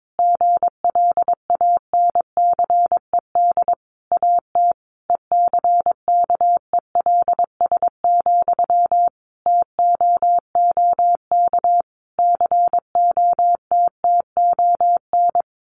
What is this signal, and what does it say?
Morse code, heard as a keyed tone, about 22 wpm, 700 Hz. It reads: GLANCED AT ECKELS, TOOK COTTON
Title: A Sound Of Thunder 22wpm